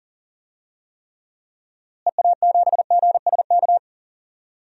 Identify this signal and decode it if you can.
Morse code, heard as a keyed tone, about 40 wpm, 700 Hz. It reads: EA7GSK
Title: DXCC-100 Call Signs - 6 Characters - Repeated 3x - 40wpm